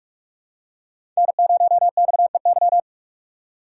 Morse code, heard as a keyed tone, about 44 wpm, 700 Hz. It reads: N0XEY
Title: USA Call Signs - 5 Characters - 6x Character Spacing - 45wpm